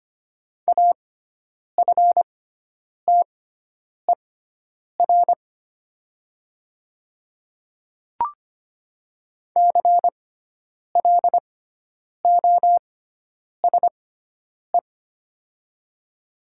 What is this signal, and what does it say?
Morse code, heard as a keyed tone, about 25 wpm, spaced as for 9 wpm, 700 Hz. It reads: AFTER CLOSE
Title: 6x Character Spacing - Top 500 Words - 25wpm